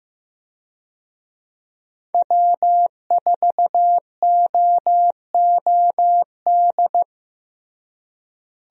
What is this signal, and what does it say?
Morse code, heard as a keyed tone, about 15 wpm, 700 Hz. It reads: W4OOD
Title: USA Call Signs - 5 Characters - Repeated 2x - 15wpm